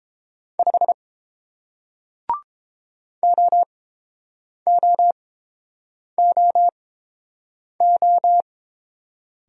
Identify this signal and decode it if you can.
Morse code, heard as a keyed tone, about 31 wpm, spaced as for 7 wpm, 700 Hz. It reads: 5 OOOO